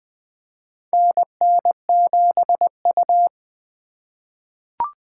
Morse code, heard as a keyed tone, about 20 wpm, 700 Hz. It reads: NN7U
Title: USA Call Signs - 4 Characters - 3x Character Spacing - 20wpm